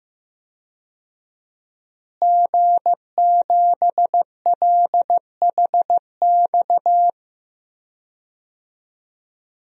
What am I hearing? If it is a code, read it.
Morse code, heard as a keyed tone, about 16 wpm, 700 Hz. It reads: G7LHX
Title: DXCC-100 Call Signs - 5 Characters - Repeated 3x - 15wpm